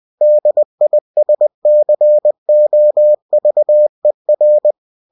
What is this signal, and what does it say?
Morse code, heard as a keyed tone, about 20 wpm, 600 Hz. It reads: DISCOVER
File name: cw-books / winnie-the-pooh / winnie-20wpm-page-0119